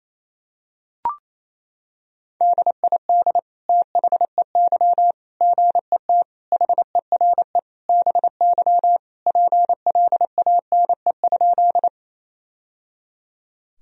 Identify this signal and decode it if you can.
Morse code, heard as a keyed tone, about 27 wpm, 700 Hz. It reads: DID THEY GET HERE BY PLANE?
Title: Sentences from Top 1000 Words - Review Part 3 - 28wpm